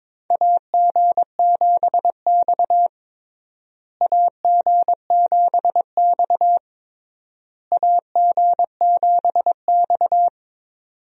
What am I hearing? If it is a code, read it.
Morse code, heard as a keyed tone, about 22 wpm, 700 Hz. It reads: AG7X AG7X AG7X
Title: USA Call Signs - 4 Characters - Repeated 3x - 22wpm